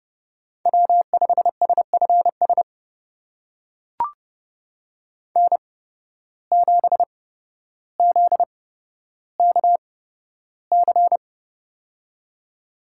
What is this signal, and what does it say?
Morse code, heard as a keyed tone, about 32 wpm, spaced as for 13 wpm, 700 Hz. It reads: W5SFS N7ZKC
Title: USA Call Signs - 5 Characters - 8x Character Spacing - 30wpm